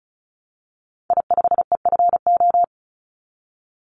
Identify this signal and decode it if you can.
Morse code, heard as a keyed tone, about 36 wpm, 700 Hz. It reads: I5EFO
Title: Call Signs - 35wpm